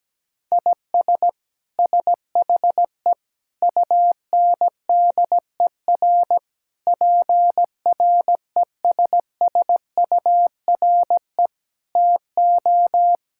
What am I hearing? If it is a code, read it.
Morse code, heard as a keyed tone, about 17 wpm, 700 Hz. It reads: IS SHE UNDER PRESSURE TO
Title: Sentences from Top 1000 Words - Review Part 6 - 17wpm